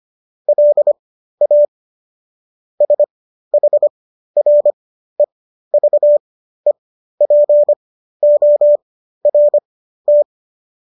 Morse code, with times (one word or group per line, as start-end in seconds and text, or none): 0.5-1.6 LA
2.8-10.2 SHREVEPORT